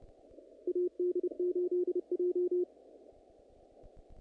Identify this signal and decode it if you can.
Morse code, heard as a keyed tone, about 30 wpm, 360 Hz. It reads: AD8J
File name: contests / naqp-cw-k5tr-op-k5pi- -1900